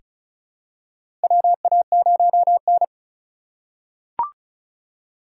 Morse code, heard as a keyed tone, about 36 wpm, 700 Hz. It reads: WA0N
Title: USA Call Signs - 3x Character Spacing - 35wpm